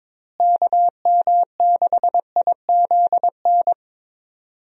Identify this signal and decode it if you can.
Morse code, heard as a keyed tone, about 22 wpm, 700 Hz. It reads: KM6IZN
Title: USA Call Signs - 6 Characters - Repeated 3x - 22wpm